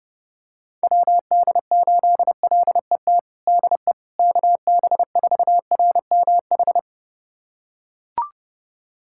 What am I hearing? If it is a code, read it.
Morse code, heard as a keyed tone, about 32 wpm, 700 Hz. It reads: WD8LET DE KB4RMH
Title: USA Callsign DE Callsign 30wpm